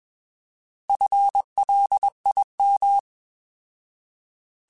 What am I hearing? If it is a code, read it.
Morse code, heard as a keyed tone, about 21 wpm, 800 Hz. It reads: FLIM
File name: Alphabetical / Alphabetical mp3s / Fs @20